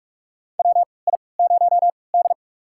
Morse code, heard as a keyed tone, about 46 wpm, spaced as for 26 wpm, 700 Hz. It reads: WI0D
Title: USA Call Signs - 4 Characters - 3x Character Spacing - 45wpm